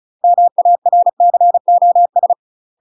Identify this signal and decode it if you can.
Morse code, heard as a keyed tone, about 36 wpm, 700 Hz. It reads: MARCOS
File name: UK MASTER QSOs TRAINING / 35WPM / UK MASTER QSOs 0026